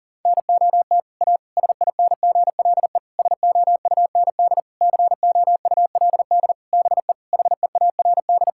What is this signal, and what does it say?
Morse code, heard as a keyed tone, about 40 wpm, 700 Hz. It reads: NOT A SINGLE SOUND COULD BE HEARD
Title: Sentences from Top 1000 Words - Review Part 5 - 40wpm